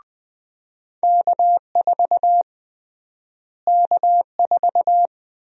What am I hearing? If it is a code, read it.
Morse code, heard as a keyed tone, about 20 wpm, 700 Hz. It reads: K4 K4
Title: USA Call Sign Prefixes - Repeated 2x - 20wpm